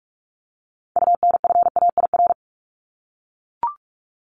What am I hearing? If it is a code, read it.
Morse code, heard as a keyed tone, about 44 wpm, 700 Hz. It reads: UNFAIR
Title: Top 1751-2000 Words 45wpm